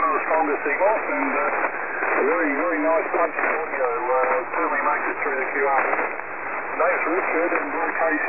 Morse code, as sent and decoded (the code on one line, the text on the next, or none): none